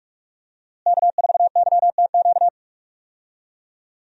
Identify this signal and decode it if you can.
Morse code, heard as a keyed tone, about 46 wpm, 700 Hz. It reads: K4YTQ
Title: USA Call Signs - 5 Characters - Repeated 3x - 45wpm